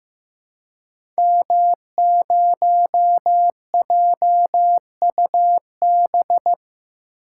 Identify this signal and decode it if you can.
Morse code, heard as a keyed tone, about 15 wpm, 700 Hz. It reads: M0JUB
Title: Code Groups - 5 Letter:Numbers - 15wpm